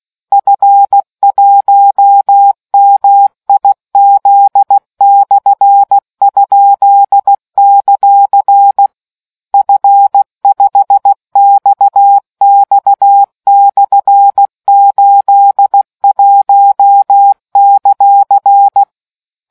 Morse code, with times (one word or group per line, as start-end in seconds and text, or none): 0.3-8.9 F1MIZ/?;
9.5-18.8 F5XX/81;